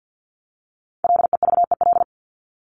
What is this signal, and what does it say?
Morse code, heard as a keyed tone, about 50 wpm, 700 Hz. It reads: LEVEL
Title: ES:S Suffix - Top 100 Words with Base Words 50wpm